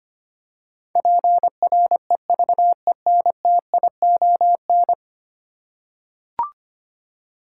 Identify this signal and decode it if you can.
Morse code, heard as a keyed tone, about 25 wpm, 700 Hz. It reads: PREVENTION
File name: Ten Letter Words - 25wpm